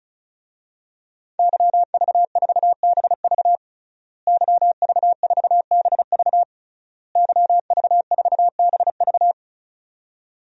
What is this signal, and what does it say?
Morse code, heard as a keyed tone, about 36 wpm, 700 Hz. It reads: YV4BV YV4BV YV4BV
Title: DXCC-100 Call Signs - 5 Characters - Repeated 3x - 35wpm